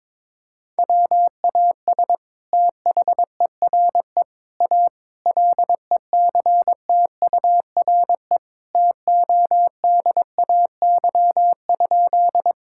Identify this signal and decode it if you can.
Morse code, heard as a keyed tone, about 22 wpm, 700 Hz. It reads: WAS THERE A LECTURE TODAY?
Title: Sentences from Top 1600 Words 22wpm